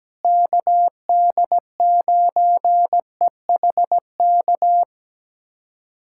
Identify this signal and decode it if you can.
Morse code, heard as a keyed tone, about 17 wpm, 700 Hz. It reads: KD9EHK
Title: USA Callsign DE Callsign - 6 Characters - 17wpm